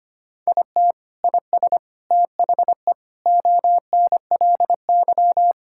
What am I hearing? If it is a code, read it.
Morse code, heard as a keyed tone, about 25 wpm, 700 Hz. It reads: IT IS THE ONLY
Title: Sentences from Top 1600 Words 25wpm